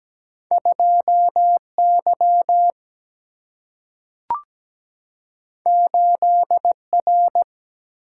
Morse code, heard as a keyed tone, about 17 wpm, 700 Hz. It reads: 2Y 8R
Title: Single-Letter Single-Number 17wpm